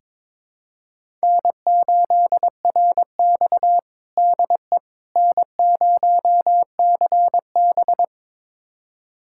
Morse code, heard as a keyed tone, about 22 wpm, 700 Hz. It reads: N8RX DE N0CB
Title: USA Callsign DE Callsign - 4 Characters - 22wpm